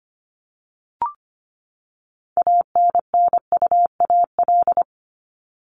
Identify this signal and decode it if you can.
Morse code, heard as a keyed tone, about 25 wpm, 700 Hz. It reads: ANNUAL
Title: Top 1251-1500 Words 25wpm